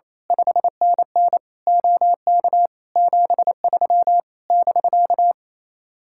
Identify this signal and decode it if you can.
Morse code, heard as a keyed tone, about 26 wpm, 700 Hz. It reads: 5NN OK 73 <BK>
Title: Speed-Racing Common QSO Phrases Encore - 28wpm